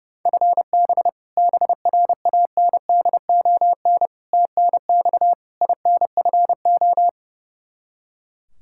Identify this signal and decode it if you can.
Morse code, heard as a keyed tone, about 30 wpm, 700 Hz. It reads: FB BRANDON TNX INFO
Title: Practice QSO - Exchange 3 - More Info - 30wpm